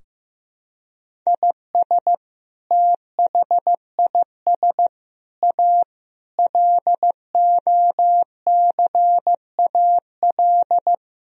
Morse code, heard as a keyed tone, about 16 wpm, 700 Hz. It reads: IS THIS A LOCAL
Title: Sentences from Top 1000 Words - Review Part 3 - 15wpm